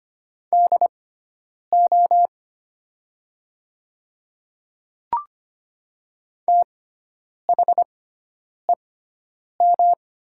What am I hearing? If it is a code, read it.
Morse code, heard as a keyed tone, about 25 wpm, spaced as for 9 wpm, 700 Hz. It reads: DO THEM
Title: 6x Character Spacing - Top 100 Words - 25wpm